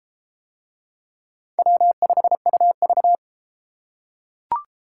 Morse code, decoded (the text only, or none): W5UV